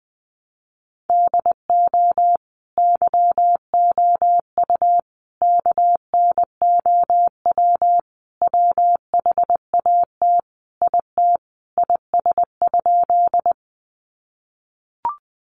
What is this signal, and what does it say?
Morse code, heard as a keyed tone, about 20 wpm, 700 Hz. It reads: DO YOU KNOW WHAT IT IS?
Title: Sentences from Top 100 Words 20wpm